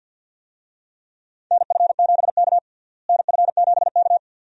Morse code, heard as a keyed tone, about 46 wpm, 700 Hz. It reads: NF7K NF7K
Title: USA Call Signs - 4 Characters - Repeated 2x - 50wpm